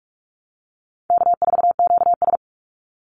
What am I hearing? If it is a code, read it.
Morse code, heard as a keyed tone, about 46 wpm, 700 Hz. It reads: K4QS
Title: NA SOTA Activator Callsigns 45wpm